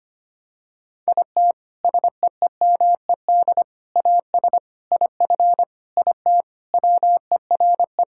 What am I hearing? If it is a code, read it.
Morse code, heard as a keyed tone, about 25 wpm, 700 Hz. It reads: IT SEEMED AS IF IT WERE
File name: practice / 25wpm / Sentences with 4-Word Phrases - 25wpm